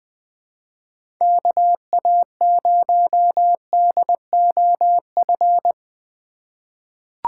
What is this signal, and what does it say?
Morse code, heard as a keyed tone, about 20 wpm, 700 Hz. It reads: KA0DOF
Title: USA Call Signs - 6 Characters - 2x Character Spacing - 20wpm